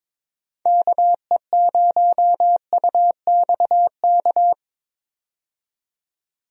Morse code, heard as a keyed tone, about 22 wpm, 700 Hz. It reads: KE0UXK
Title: USA Call Signs - 6 Characters - 22wpm